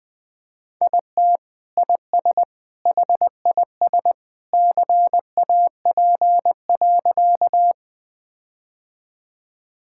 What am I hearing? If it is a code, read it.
Morse code, heard as a keyed tone, about 20 wpm, 700 Hz. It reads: IT IS HIS CAP.
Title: Sentences from Top 1600 Words 20wpm